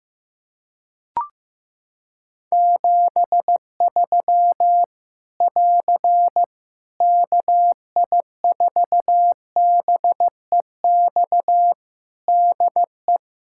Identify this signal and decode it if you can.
Morse code, heard as a keyed tone, about 15 wpm, 700 Hz. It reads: 73 <AR> KI4BEX DE
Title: Practice QSO - Exchange 4 - Ending - 15wpm